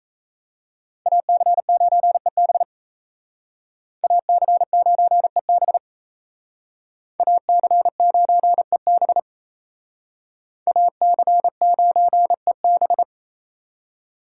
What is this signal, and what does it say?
Morse code, heard as a keyed tone, about 37 wpm, 700 Hz. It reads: AC9EB AC9EB AC9EB AC9EB